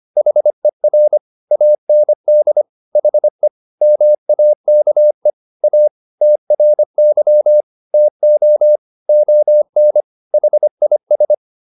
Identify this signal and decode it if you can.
Morse code, heard as a keyed tone, about 25 wpm, 600 Hz. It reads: HER AND HE MAKE A TRY TO ON HIS